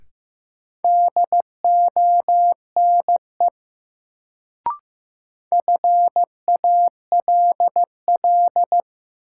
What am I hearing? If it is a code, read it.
Morse code, heard as a keyed tone, about 15 wpm, 700 Hz. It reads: DONE FALL
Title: Sentences from Top 500 Words - New Word Drill - 15wpm